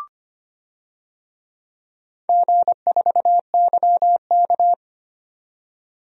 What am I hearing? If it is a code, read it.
Morse code, heard as a keyed tone, about 25 wpm, 700 Hz. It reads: G4YK
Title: DXCC-100 Call Signs - 4 Characters - 25wpm